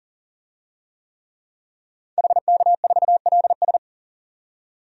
Morse code, heard as a keyed tone, about 40 wpm, 700 Hz. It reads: HK4LS
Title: DXCC-100 Call Signs - 5 Characters - 40wpm